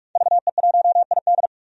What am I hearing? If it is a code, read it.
Morse code, heard as a keyed tone, about 46 wpm, 700 Hz. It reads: VE1ID
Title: DXCC-100 Call Signs - 5 Characters - 45wpm